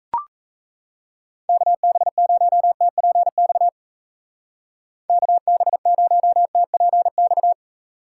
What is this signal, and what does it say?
Morse code, heard as a keyed tone, about 34 wpm, 700 Hz. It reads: KB0TPX KB0TPX
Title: Speed Racing - USA 2x3 Call Signs 28wpm